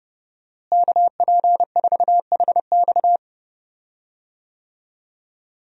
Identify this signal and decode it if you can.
Morse code, heard as a keyed tone, about 32 wpm, 700 Hz. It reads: KP4HX
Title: USA Call Signs - 5 Characters - Repeated 3x - 30wpm